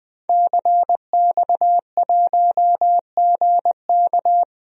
Morse code, heard as a keyed tone, about 20 wpm, 700 Hz. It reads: CX1GK
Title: DXCC-100 Call Signs - 5 Characters - 20wpm